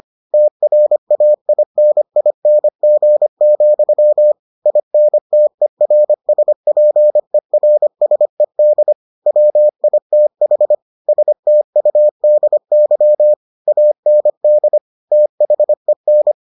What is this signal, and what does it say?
Morse code, heard as a keyed tone, about 25 wpm, 600 Hz. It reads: TRAINING, INTERSPERSED WITH STUDY AND THEN